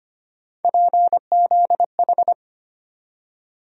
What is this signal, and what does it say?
Morse code, heard as a keyed tone, about 25 wpm, 700 Hz. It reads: PZH
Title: USA Call Sign Postfixes - 3 Letters 25wpm